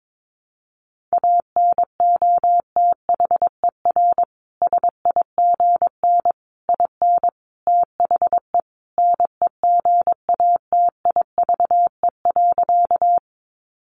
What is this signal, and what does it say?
Morse code, heard as a keyed tone, about 22 wpm, 700 Hz. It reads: ANOTHER SIGN IN THE NEGATIVE.
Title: Sentences from Top 1000 Words 22wpm